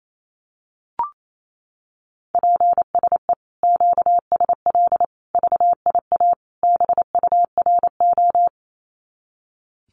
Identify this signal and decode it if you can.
Morse code, heard as a keyed tone, about 28 wpm, 700 Hz. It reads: PSE QSL VIA BURO
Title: Common QSO Phrases 28wpm